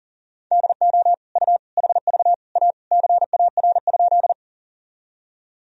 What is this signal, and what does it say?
Morse code, heard as a keyed tone, about 37 wpm, 700 Hz. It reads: DO U HV A CAR?
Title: Sentences from Top 300 Words and Abbreviations 40wpm